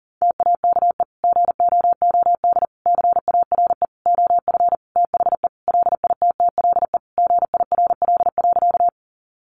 Morse code, heard as a keyed tone, about 40 wpm, 700 Hz. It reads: TAKE GOOD CARE OF THE LITTLE GIRL.
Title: Sentences from Top 400 Words 40wpm